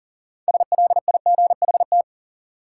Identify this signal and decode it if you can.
Morse code, heard as a keyed tone, about 40 wpm, 700 Hz. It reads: SLIGHT
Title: Six Letter Words - 40wpm